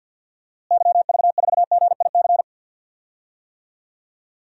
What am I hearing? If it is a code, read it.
Morse code, heard as a keyed tone, about 48 wpm, 700 Hz. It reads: YV4GIC
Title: DXCC-100 Call Signs - 6 Characters - Repeated 3x - 50wpm